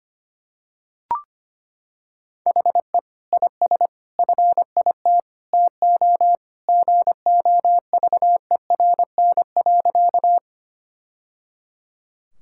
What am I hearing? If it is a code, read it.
Morse code, heard as a keyed tone, about 25 wpm, 700 Hz. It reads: HE IS FIT TO GOVERN.
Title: Sentences from Top 1000 Words - Review Part 5 - 25wpm